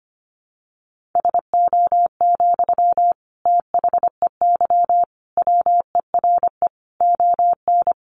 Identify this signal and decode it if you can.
Morse code, heard as a keyed tone, about 25 wpm, 700 Hz. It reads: SO, THEY WERE ON
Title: Sentences from Top 1400 Words 25wpm